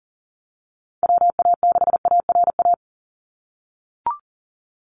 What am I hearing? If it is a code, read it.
Morse code, heard as a keyed tone, about 40 wpm, 700 Hz. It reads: WA6ARA
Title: NA SOTA Chaser Callsigns 40wpm